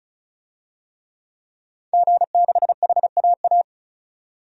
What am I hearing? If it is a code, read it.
Morse code, heard as a keyed tone, about 34 wpm, 700 Hz. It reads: G6HAA
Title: DXCC-100 Call Signs - 5 Characters - 35wpm